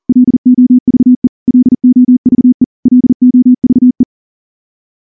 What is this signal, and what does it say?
Morse code, heard as a keyed tone, about 38 wpm, 250 Hz. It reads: LOVE LOVE LOVE
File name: morse / morse-training-40WPM